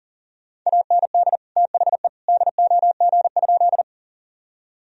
Morse code, heard as a keyed tone, about 40 wpm, 700 Hz. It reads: AND THE DOG?
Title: Sentences from Top 1000 Words - Review Part 2 - 40wpm